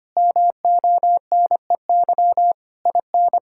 Morse code, heard as a keyed tone, about 25 wpm, 700 Hz. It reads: MONEY IN
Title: Sentences from Top 1000 Words - Review Part 5 - 25wpm